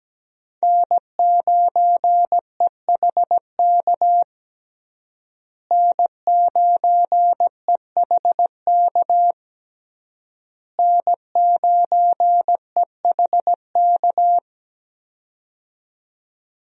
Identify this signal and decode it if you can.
Morse code, heard as a keyed tone, about 17 wpm, 700 Hz. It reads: N9EHK N9EHK N9EHK
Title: USA Call Signs - 5 Characters - Repeated 3x - 17wpm